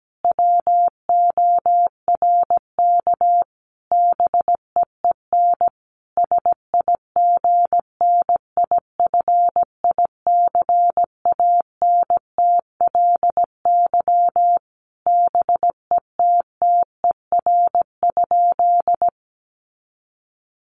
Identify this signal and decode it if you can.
Morse code, heard as a keyed tone, about 17 wpm, 700 Hz. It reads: WORK BEEN SIGNIFICANTLY BETTER?
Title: Sentences from Top 1500 Words 17wpm